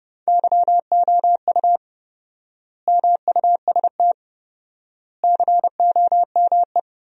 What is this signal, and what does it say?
Morse code, heard as a keyed tone, about 31 wpm, 700 Hz. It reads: YOU MUST COME
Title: 4x Word-Spacing - Sentences from Top 200 Words - 30wpm